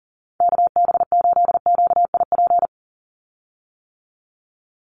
Morse code, heard as a keyed tone, about 40 wpm, 700 Hz. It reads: KB8QIP
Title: USA Call Signs 40wpm